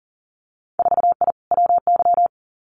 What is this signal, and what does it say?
Morse code, heard as a keyed tone, about 37 wpm, 700 Hz. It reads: 4I WY
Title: Winter Field Day Exchanges Encore 40wpm